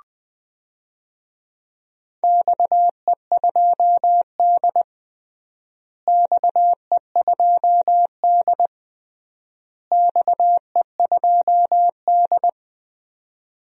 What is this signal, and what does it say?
Morse code, heard as a keyed tone, about 20 wpm, 700 Hz. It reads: XE2D XE2D XE2D
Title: DXCC-100 Call Signs - 4 Characters - Repeated 3x - 20wpm